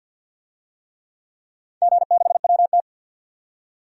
Morse code, heard as a keyed tone, about 47 wpm, 700 Hz. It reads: G6WT
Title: DXCC-100 Call Signs - 4 Characters - 50wpm